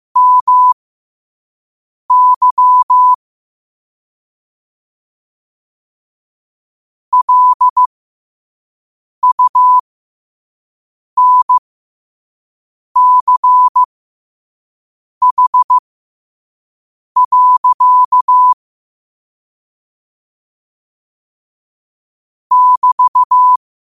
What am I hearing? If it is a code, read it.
Morse code, heard as a keyed tone, about 15 wpm, spaced as for 5 wpm, 1000 Hz. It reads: MY LUNCH. <BT>
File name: qotdfiles / 5wpm